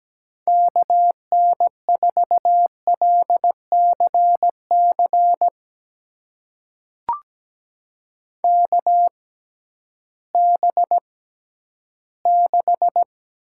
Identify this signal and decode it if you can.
Morse code, heard as a keyed tone, about 17 wpm, spaced as for 11 wpm, 700 Hz. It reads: KN4LCC KB6
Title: USA Call Signs - 6 Characters - 6x Character Spacing - 17wpm